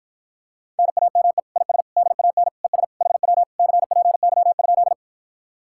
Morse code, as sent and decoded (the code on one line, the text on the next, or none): -. .- -- . / .. ... / -.. .- -. / . ... / .... .-- / -.-. .--. -.-- ..--..
NAME IS DAN ES HW CPY?